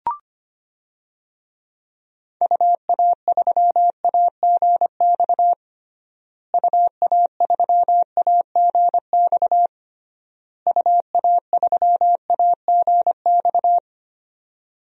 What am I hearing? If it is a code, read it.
Morse code, heard as a keyed tone, about 25 wpm, 700 Hz. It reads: UA3AGX UA3AGX UA3AGX
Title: DXCC-100 Call Signs - 6 Characters - Repeated 3x - 25wpm